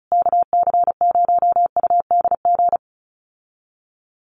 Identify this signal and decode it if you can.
Morse code, heard as a keyed tone, about 34 wpm, 700 Hz. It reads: KC0UDG
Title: USA Call Signs Repeated 3x 35wpm